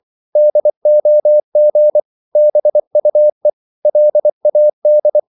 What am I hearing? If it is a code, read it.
Morse code, heard as a keyed tone, about 24 wpm, 600 Hz. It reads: DOG BUE LAD
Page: Korte danske ord 24wpm